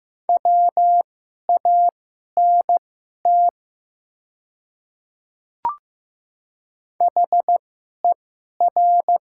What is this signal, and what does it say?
Morse code, heard as a keyed tone, about 15 wpm, spaced as for 11 wpm, 700 Hz. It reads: WANT HER
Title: 2x Character Spacing - Top 200 Words - 15wpm